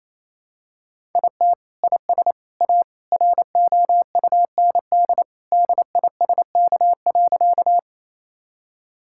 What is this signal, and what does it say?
Morse code, heard as a keyed tone, about 28 wpm, 700 Hz. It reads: IT IS A ROUND DISK.
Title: Sentences from Top 1000 Words - Review Part 6 - 28wpm